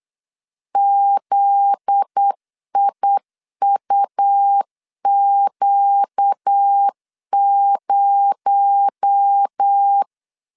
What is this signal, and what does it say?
Morse code, heard as a keyed tone, about 8 wpm, 800 Hz. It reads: ZIUQ0
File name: LEZIONI / 8 YJ45 / 8 YJ45 8wpm 3214